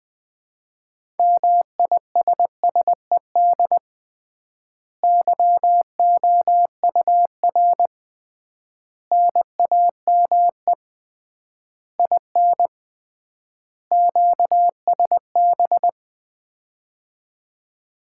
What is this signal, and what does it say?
Morse code, heard as a keyed tone, about 21 wpm, 700 Hz. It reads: MISSED YOUR NAME IN QSB
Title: Common QSO Phrases - 3x Word-Spacing - 20wpm